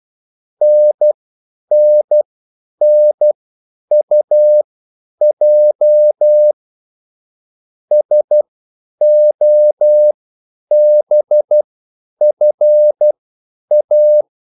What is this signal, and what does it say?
Morse code, heard as a keyed tone, about 12 wpm, spaced as for 9 wpm, 600 Hz. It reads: NNNUJ SOBFA
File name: QSL / CW lesson